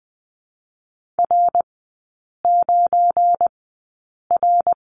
Morse code, heard as a keyed tone, about 20 wpm, spaced as for 8 wpm, 690 Hz. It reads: R9R
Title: 172 Lesson 28e - Number 9 - Three Characters 20wpm